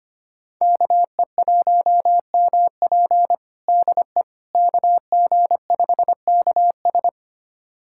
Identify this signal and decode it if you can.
Morse code, heard as a keyed tone, about 25 wpm, 700 Hz. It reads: KE1MP DE KG5KS
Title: USA Callsign DE Callsign - 5 Characters - 25wpm